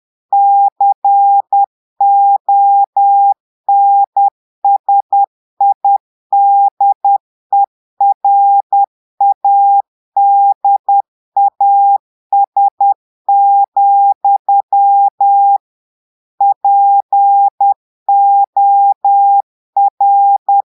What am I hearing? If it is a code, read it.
Morse code, heard as a keyed tone, about 10 wpm, 800 Hz. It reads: CONSIDERADAS, POR
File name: portugais texte en clair 2 vitesse 10 mots minutes